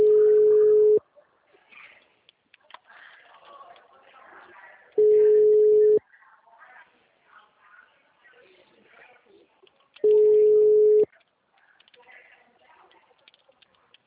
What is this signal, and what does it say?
Ringback tone (Europe, 425 Hz), with speech.
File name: out- -2007- - - .0